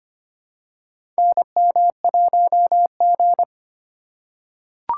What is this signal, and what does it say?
Morse code, heard as a keyed tone, about 25 wpm, 700 Hz. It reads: NM1G
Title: USA Call Signs - 4 Characters - 2x Character Spacing - 25wpm